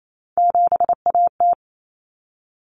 Morse code, heard as a keyed tone, about 28 wpm, 700 Hz. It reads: 7AT
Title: Common Cut Numbers - 3-Digit 28wpm